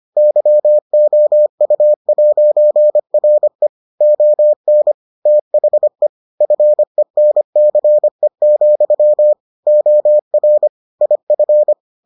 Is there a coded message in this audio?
Yes: Morse code from 0.2 to 11.7 s, heard as a keyed tone, about 25 wpm, 600 Hz. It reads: YOU'RE ON THE FENCE, OR IF